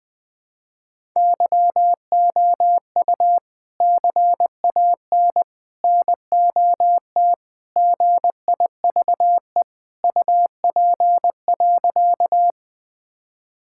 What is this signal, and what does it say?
Morse code, heard as a keyed tone, about 20 wpm, 700 Hz. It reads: YOU CAN NOT GIVE UP.
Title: Sentences with 3-Word Phrases - 20wpm